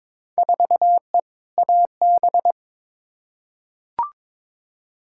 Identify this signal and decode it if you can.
Morse code, heard as a keyed tone, about 21 wpm, 700 Hz. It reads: 4E AB
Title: Field Day Exchanges Encore - 22wpm